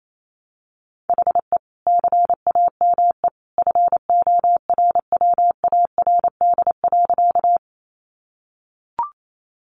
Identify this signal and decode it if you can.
Morse code, heard as a keyed tone, about 28 wpm, 700 Hz. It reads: HE CAME FORWARD.
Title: Sentences from Top 1300 Words 28wpm